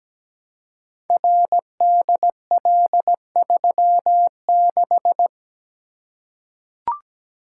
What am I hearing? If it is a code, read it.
Morse code, heard as a keyed tone, about 17 wpm, 700 Hz. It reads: RDL36
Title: Code Groups - 5 Letter:Numbers - 17wpm